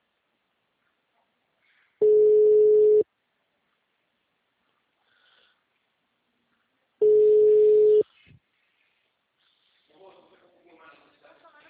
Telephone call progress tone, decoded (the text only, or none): ringback tone